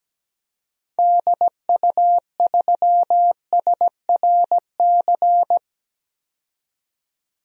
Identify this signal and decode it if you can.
Morse code, heard as a keyed tone, about 17 wpm, 700 Hz. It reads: DU3SRC
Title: DXCC-100 Call Signs - 6 Characters - 17wpm